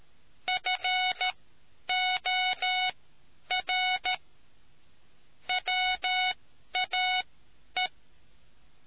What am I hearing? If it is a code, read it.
Morse code, heard as a keyed tone, about 13 wpm, spaced as for 10 wpm, 740 Hz. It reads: FOR WAE